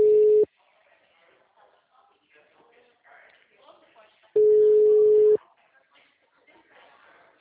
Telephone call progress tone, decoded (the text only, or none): ringback tone